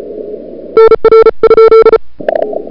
Morse code, heard as a keyed tone, about 37 wpm, 1340 Hz. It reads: NR ?
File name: CWSprint / February 2007 CW Sprint / 01-2007 CW Sprint Hour1